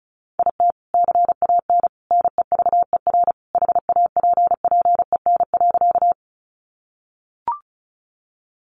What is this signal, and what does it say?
Morse code, heard as a keyed tone, about 34 wpm, 700 Hz. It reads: IT CAN NEVER HAPPEN.
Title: Sentences from Top 400 Words 35wpm